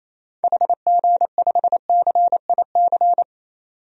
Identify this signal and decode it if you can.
Morse code, heard as a keyed tone, about 28 wpm, 700 Hz. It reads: HG5CIC
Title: DXCC-100 Call Signs - 6 Characters - 28wpm